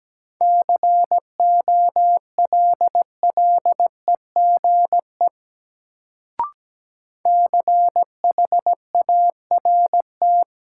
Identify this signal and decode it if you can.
Morse code, heard as a keyed tone, about 17 wpm, 700 Hz. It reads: COLLEGE CHART
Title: Sentences from Top 1300 Words - New Word Drill Alt - 17wpm